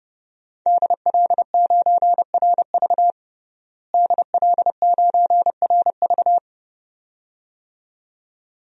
Morse code, heard as a keyed tone, about 32 wpm, 700 Hz. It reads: DL9RV DL9RV
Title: DXCC-100 Call Signs - 5 Characters - Repeated 2x - 30wpm